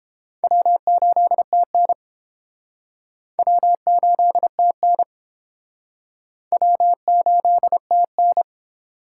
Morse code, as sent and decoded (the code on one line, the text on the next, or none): .-- ---.. - -. / .-- ---.. - -. / .-- ---.. - -.
W8TN W8TN W8TN